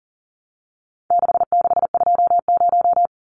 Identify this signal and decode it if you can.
Morse code, heard as a keyed tone, about 40 wpm, 700 Hz. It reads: 6620
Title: Numbers - 4-Digit 40wpm